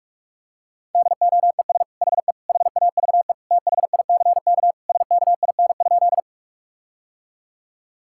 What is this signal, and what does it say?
Morse code, heard as a keyed tone, about 46 wpm, 700 Hz. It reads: DOES HE HAVE THICK SKIN?
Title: Sentences from Top 1600 Words 45wpm